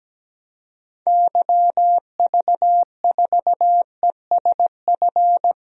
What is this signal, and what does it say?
Morse code, heard as a keyed tone, about 17 wpm, 700 Hz. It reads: YV4ESF
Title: DXCC-100 Call Signs - 6 Characters - Repeated 3x - 17wpm